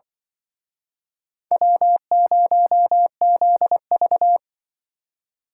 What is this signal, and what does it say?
Morse code, heard as a keyed tone, about 24 wpm, 700 Hz. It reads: W0ZV
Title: Speed Racing - USA 1x2 Call Signs 20wpm